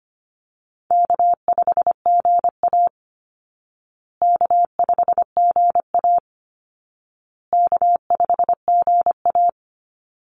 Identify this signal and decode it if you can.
Morse code, heard as a keyed tone, about 25 wpm, 700 Hz. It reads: K5GA K5GA K5GA
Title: Easy Call Signs Repeated 3x - 25wpm